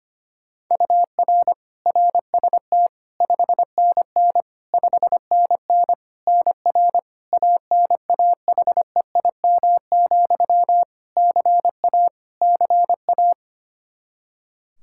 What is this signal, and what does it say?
Morse code, heard as a keyed tone, about 25 wpm, 700 Hz. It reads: UR RST 5NN 5NN NR ANAHEIM, CA CA
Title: Practice QSO - Exchange 1 - Beginning - 25wpm